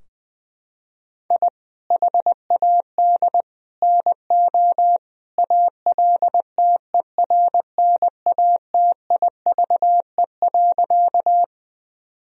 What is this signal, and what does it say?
Morse code, heard as a keyed tone, about 20 wpm, 700 Hz. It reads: I HAD NO ALTERNATIVE.
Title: Sentences from Top 1000 Words - Review Part 4 - 20wpm